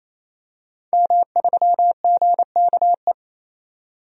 Morse code, heard as a keyed tone, about 28 wpm, 700 Hz. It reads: M3GKE
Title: DXCC-100 Call Signs - 5 Characters - 28wpm